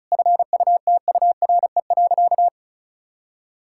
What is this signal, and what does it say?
Morse code, heard as a keyed tone, about 36 wpm, 700 Hz. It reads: FUTURE.
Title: Sentences from Top 1000 Words - Review Part 6 - 35wpm